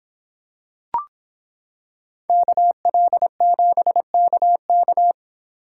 Morse code, heard as a keyed tone, about 26 wpm, 700 Hz. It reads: KL7KK